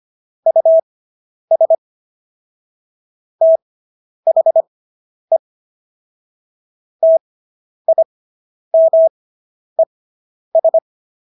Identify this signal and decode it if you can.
Morse code, heard as a keyed tone, about 25 wpm, spaced as for 10 wpm, 650 Hz. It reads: US THE TIMES